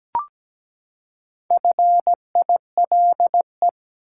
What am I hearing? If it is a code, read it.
Morse code, heard as a keyed tone, about 17 wpm, 700 Hz. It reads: FILE
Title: Four Letter Words - 17wpm